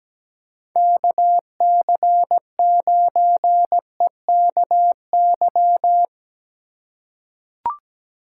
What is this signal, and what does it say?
Morse code, heard as a keyed tone, about 17 wpm, 700 Hz. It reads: KC9EKY
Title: USA Call Signs - 6 Characters - 3x Character Spacing - 17wpm